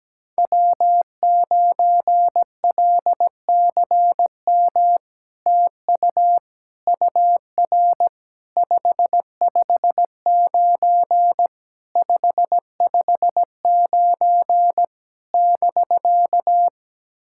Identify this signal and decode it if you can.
Morse code, heard as a keyed tone, about 17 wpm, 700 Hz. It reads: W9LCM TU UR 559 559 <BK>
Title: SOTA QSO Practice 17wpm